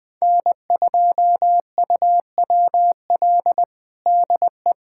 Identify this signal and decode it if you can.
Morse code, heard as a keyed tone, about 20 wpm, 700 Hz. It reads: N2UWL DE
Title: USA Callsign DE Callsign - 5 Characters - 20wpm